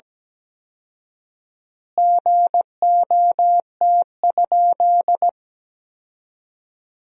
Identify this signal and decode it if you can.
Morse code, heard as a keyed tone, about 17 wpm, 700 Hz. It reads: GOT?
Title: 4x Word-Spacing - Sentences from Top 300 Words - 17wpm